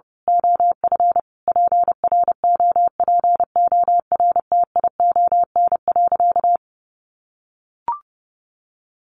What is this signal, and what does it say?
Morse code, heard as a keyed tone, about 30 wpm, 700 Hz. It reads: OF PROPORTION.
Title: A Sound Of Thunder 30wpm